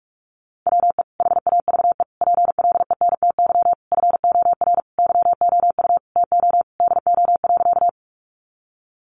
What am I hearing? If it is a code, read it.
Morse code, heard as a keyed tone, about 44 wpm, 700 Hz. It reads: WE HAVE PLENTY FOR YOU TO DO.
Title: Sentences from Top 1300 Words 45wpm